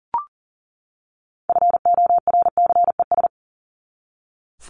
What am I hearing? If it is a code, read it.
Morse code, heard as a keyed tone, about 40 wpm, 700 Hz. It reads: FORCES
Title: ES:S Suffix - Top 100 Words with Base Words 40wpm